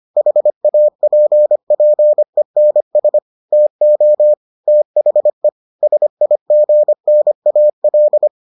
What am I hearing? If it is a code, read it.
Morse code, heard as a keyed tone, about 25 wpm, 600 Hz. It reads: HAPPENS TO THE SIGNAL